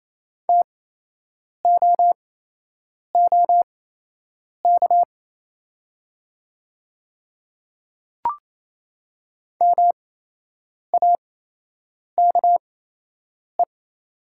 Morse code, heard as a keyed tone, about 28 wpm, spaced as for 8 wpm, 700 Hz. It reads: TOOK MAKE
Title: 8x Character Spacing - Top 500 Words - 28wpm